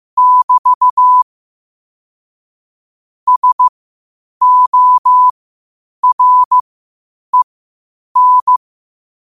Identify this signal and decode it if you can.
Morse code, heard as a keyed tone, about 15 wpm, spaced as for 8 wpm, 1000 Hz. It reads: <BT> SOREN
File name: qotdfiles / 8wpm